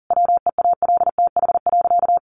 Morse code, heard as a keyed tone, about 37 wpm, 700 Hz. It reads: WEALTH.
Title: Sentences from Top 1300 Words 40wpm